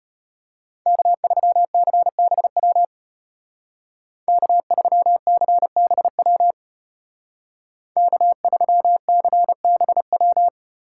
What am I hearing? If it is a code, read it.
Morse code, heard as a keyed tone, about 34 wpm, 700 Hz. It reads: K3CBW K3CBW K3CBW